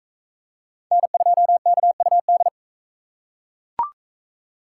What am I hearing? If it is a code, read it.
Morse code, heard as a keyed tone, about 44 wpm, 700 Hz. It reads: N2KUD